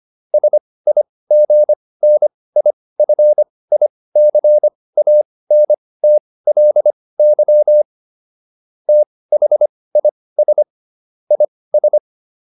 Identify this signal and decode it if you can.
Morse code, heard as a keyed tone, about 25 wpm, spaced as for 18 wpm, 600 Hz. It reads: SIGNIFICANTLY THIS IS